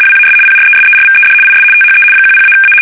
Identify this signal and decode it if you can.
RTTY, about 75 baud. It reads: WAS PRESERVED IS THE 10-A
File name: fsk850-75B / teletype-corp